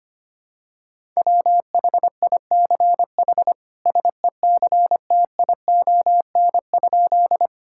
Morse code, heard as a keyed tone, about 25 wpm, 700 Hz. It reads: WHICH SECTION?
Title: Sentences from Top 1000 Words - Review Part 5 - 25wpm